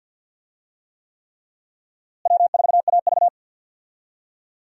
Morse code, heard as a keyed tone, about 50 wpm, 700 Hz. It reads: W4AV
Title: USA Call Signs - 4 Characters - Repeated 3x - 50wpm